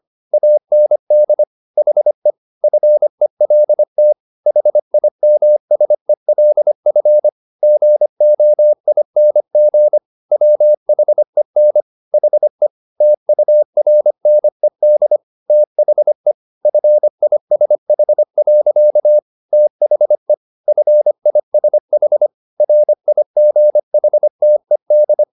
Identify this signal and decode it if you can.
Morse code, heard as a keyed tone, about 25 wpm, 600 Hz. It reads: AND HE FELT HIMSELF GOING WHEN HE TURNED THE FISH. THE FISH RIGHTED